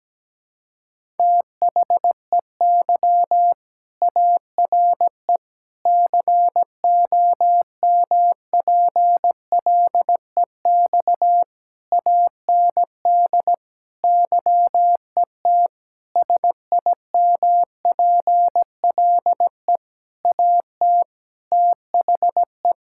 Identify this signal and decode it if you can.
Morse code, heard as a keyed tone, about 17 wpm, 700 Hz. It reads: THEY ARE COMPLEX AND YET SIMPLE AT THE
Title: Sentences from Top 1000 Words - Review Part 6 - 17wpm